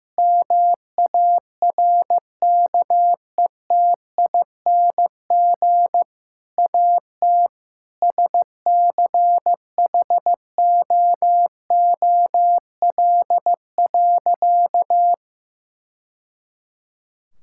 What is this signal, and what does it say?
Morse code, heard as a keyed tone, about 15 wpm, 700 Hz. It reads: MARKETING AT SCHOOL.
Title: Sentences from Top 1000 Words - Review Part 5 - 15wpm